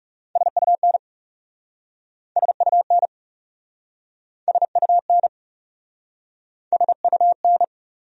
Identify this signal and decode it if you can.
Morse code, heard as a keyed tone, about 37 wpm, 700 Hz. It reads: SUN SUN SUN SUN
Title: Speed-Racing Top 300 Words 30wpm